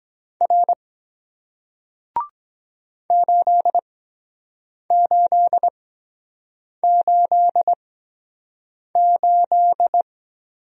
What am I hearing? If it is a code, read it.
Morse code, heard as a keyed tone, about 23 wpm, spaced as for 7 wpm, 700 Hz. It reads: R 8888